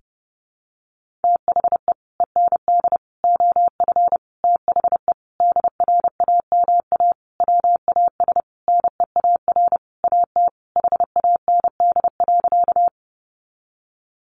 Morse code, heard as a keyed tone, about 30 wpm, 700 Hz. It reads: THE END OF THE DRAMA WAS NEAR AT HAND.
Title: Sentences from Top 1500 Words 30wpm